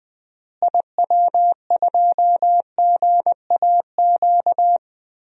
Morse code, heard as a keyed tone, about 21 wpm, 700 Hz. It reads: IW2GAQ
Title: DXCC-100 Call Signs - 6 Characters - Repeated 3x - 20wpm